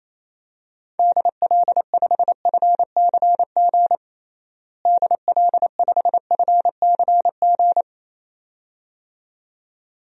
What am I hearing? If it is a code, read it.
Morse code, heard as a keyed tone, about 28 wpm, 700 Hz. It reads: DL5FCG DL5FCG
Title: DXCC-100 Call Signs - 6 Characters - Repeated 2x - 28wpm